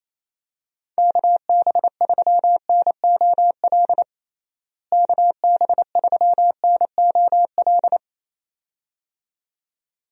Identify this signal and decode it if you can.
Morse code, heard as a keyed tone, about 28 wpm, 700 Hz. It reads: KB3NOL KB3NOL
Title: USA Call Signs - 6 Characters - Repeated 2x - 28wpm